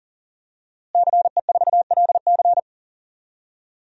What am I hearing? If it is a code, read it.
Morse code, heard as a keyed tone, about 40 wpm, 700 Hz. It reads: CE4LC
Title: DXCC-100 Call Signs - 5 Characters - Repeated 2x - 40wpm